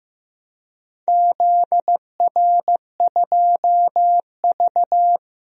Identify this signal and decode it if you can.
Morse code, heard as a keyed tone, about 15 wpm, 700 Hz. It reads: ZR2V
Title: DXCC-100 Call Signs - 4 Characters - 15wpm